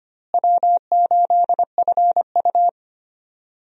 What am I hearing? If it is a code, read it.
Morse code, heard as a keyed tone, about 25 wpm, 700 Hz. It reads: W8FU
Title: USA Call Signs - 4 Characters - Repeated 2x - 25wpm